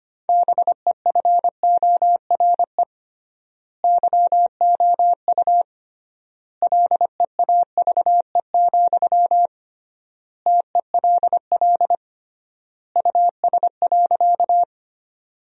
Morse code, heard as a keyed tone, about 25 wpm, 700 Hz. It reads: BEFORE YOU LEAVE, TELL US.
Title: 3x Word-Spacing - Sentences from Top 400 Words - 25wpm